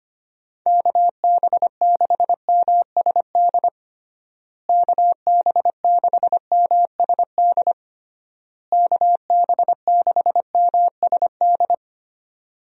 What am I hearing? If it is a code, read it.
Morse code, heard as a keyed tone, about 25 wpm, 700 Hz. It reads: KB6MSD KB6MSD KB6MSD
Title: USA Call Signs - 6 Characters - Repeated 3x - 25wpm